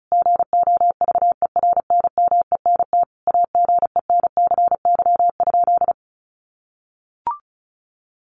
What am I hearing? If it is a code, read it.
Morse code, heard as a keyed tone, about 34 wpm, 700 Hz. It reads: GOVERNMENT AGENCY?
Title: Sentences from Top 1300 Words 35wpm